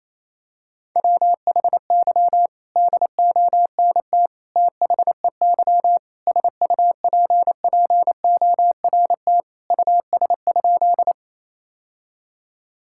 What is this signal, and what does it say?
Morse code, heard as a keyed tone, about 28 wpm, 700 Hz. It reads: WHY DONT THEY SUPPORT US?
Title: Sentences from Top 1000 Words - Review Part 5 - 28wpm